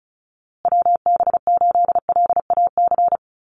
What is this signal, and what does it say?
Morse code, heard as a keyed tone, about 34 wpm, 700 Hz. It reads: WB8LAC
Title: USA Call Signs Repeated 2x 35wpm